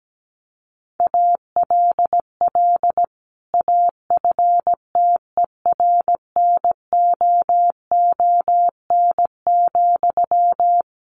Morse code, heard as a keyed tone, about 17 wpm, 700 Hz. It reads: ALL AFTERNOON,
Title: Sentences from Top 1500 Words 17wpm